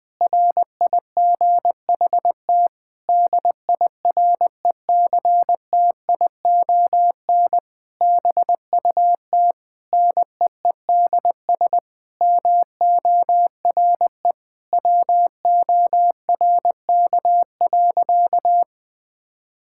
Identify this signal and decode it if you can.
Morse code, heard as a keyed tone, about 20 wpm, 700 Hz. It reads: RIGHT DIRECTION BUT NEEDS MORE WORK.
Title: Sentences with 8-Word Phrases - Spoken Beginning - 20wpm